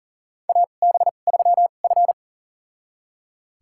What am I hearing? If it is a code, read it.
Morse code, heard as a keyed tone, about 40 wpm, spaced as for 29 wpm, 700 Hz. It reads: AB3F
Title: USA Call Signs - 4 Characters - 2x Character Spacing - 40wpm